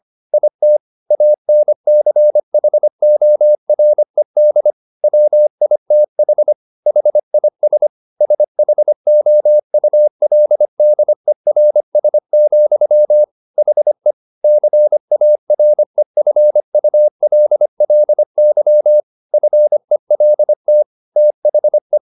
Morse code, heard as a keyed tone, about 25 wpm, 600 Hz. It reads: IT ANCHORED WITH HIS SHOULDERS, HE CAREFULLY FELT THE
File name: the old man and the sea part 1 25wpm